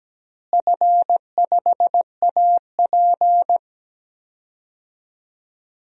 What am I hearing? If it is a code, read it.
Morse code, heard as a keyed tone, about 17 wpm, 700 Hz. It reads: F5AP
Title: DXCC-100 Call Signs - 4 Characters - Repeated 3x - 17wpm